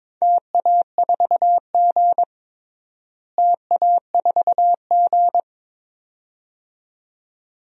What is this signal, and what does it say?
Morse code, heard as a keyed tone, about 21 wpm, 700 Hz. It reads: TA4G TA4G
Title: DXCC-100 Call Signs - 4 Characters - Repeated 2x - 22wpm